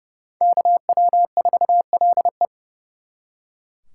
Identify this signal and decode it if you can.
Morse code, heard as a keyed tone, about 36 wpm, 700 Hz. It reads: KW4LE
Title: Speed Racing - USA 2x2 Call Signs 30wpm